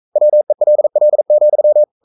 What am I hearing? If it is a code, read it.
Morse code, heard as a keyed tone, about 42 wpm, 600 Hz. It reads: WELL,
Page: Romanen Never af Ken Follet på engelsk.